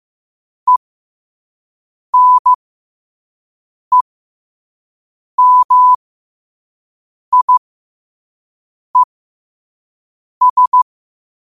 Morse code, heard as a keyed tone, about 15 wpm, spaced as for 5 wpm, 1000 Hz. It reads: ENEMIES
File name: qotdfiles / 5wpm